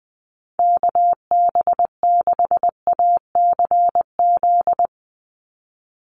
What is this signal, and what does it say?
Morse code, heard as a keyed tone, about 20 wpm, 700 Hz. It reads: KB6ACZ
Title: USA Call Signs 20wpm